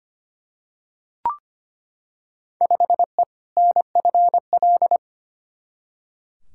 Morse code, heard as a keyed tone, about 25 wpm, 700 Hz. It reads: 5E NFL
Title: Field Day Exchanges Encore - 25wpm